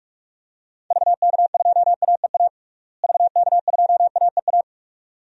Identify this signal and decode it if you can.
Morse code, heard as a keyed tone, about 42 wpm, 700 Hz. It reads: VK2AEA VK2AEA
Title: DXCC-100 Call Signs - 6 Characters - Repeated 2x - 45wpm